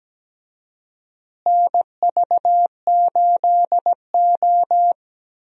Morse code, heard as a keyed tone, about 17 wpm, 700 Hz. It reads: NV8O
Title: USA Call Signs - 4 Characters - Repeated 3x - 17wpm